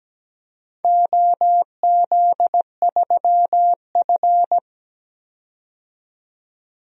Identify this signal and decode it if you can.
Morse code, heard as a keyed tone, about 17 wpm, 700 Hz. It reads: OZ3F
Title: DXCC-100 Call Signs - 4 Characters - 17wpm